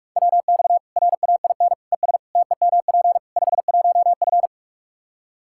Morse code, heard as a keyed tone, about 44 wpm, 700 Hz. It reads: WX RAIN ES TEMP 51F
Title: Practice QSO - Exchange 2 - Info - 45wpm